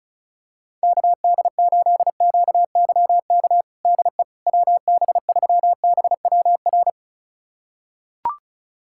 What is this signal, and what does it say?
Morse code, heard as a keyed tone, about 36 wpm, 700 Hz. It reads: KD8QYK DE WB3BWR
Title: USA Callsign DE Callsign 35wpm